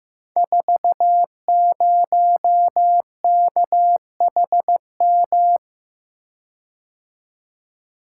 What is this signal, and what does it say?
Morse code, heard as a keyed tone, about 15 wpm, 700 Hz. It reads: 40KHM
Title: Code Groups - 5 Letter:Numbers - 15wpm